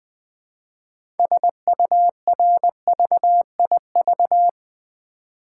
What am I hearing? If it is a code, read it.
Morse code, heard as a keyed tone, about 20 wpm, 700 Hz. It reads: SURVIV
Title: Build A Word - 2 Letters At A Time - 8-Letter Words - 20wpm